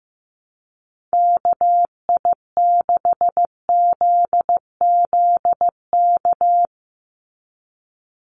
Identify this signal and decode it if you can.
Morse code, heard as a keyed tone, about 15 wpm, 700 Hz. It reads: KI6ZZK
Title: USA Call Signs 15wpm